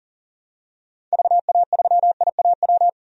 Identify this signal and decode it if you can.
Morse code, heard as a keyed tone, about 40 wpm, 700 Hz. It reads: VA3IAW
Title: DXCC-100 Call Signs - 6 Characters - Repeated 3x - 40wpm